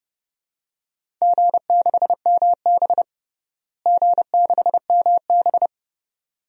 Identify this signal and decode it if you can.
Morse code, heard as a keyed tone, about 32 wpm, 700 Hz. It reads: G6MB G6MB
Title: DXCC-100 Call Signs - 4 Characters - Repeated 2x - 30wpm